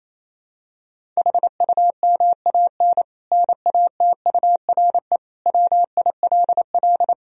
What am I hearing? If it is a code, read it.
Morse code, heard as a keyed tone, about 28 wpm, 700 Hz. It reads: HUMAN NATURE WILL
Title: Sentences from Top 1000 Words - Review Part 3 - 28wpm